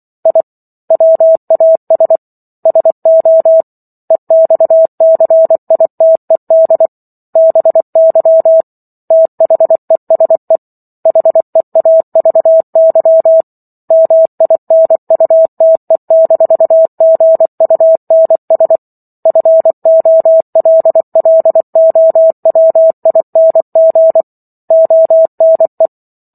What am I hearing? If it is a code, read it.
Morse code, heard as a keyed tone, about 24 wpm, 650 Hz. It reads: I WAS SO EXCITED BY THESE HEAVY MINUTE-GUNS FOLLOWING ONE